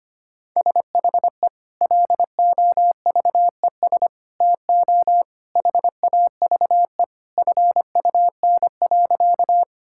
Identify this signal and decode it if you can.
Morse code, heard as a keyed tone, about 25 wpm, 690 Hz. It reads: SHE LOVES TO HAVE FUN.
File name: practice / 25wpm / Sentences with 2-Word Phrases - 25wpm